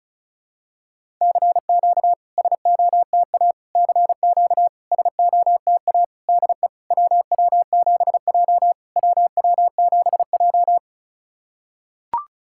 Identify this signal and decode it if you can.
Morse code, heard as a keyed tone, about 36 wpm, 690 Hz. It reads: CQ SOTA CQ SOTA DE WW7J WW7J
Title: SOTA QSO Practice 35wpm